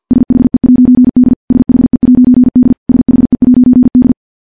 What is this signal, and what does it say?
Morse code, heard as a keyed tone, about 50 wpm, 250 Hz. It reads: SHE'D SHE'D SHE'D
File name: morse / morse-training-50WPM